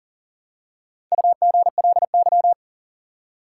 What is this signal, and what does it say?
Morse code, heard as a keyed tone, about 40 wpm, 700 Hz. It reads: UGLY
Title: Four Letter Words - 40wpm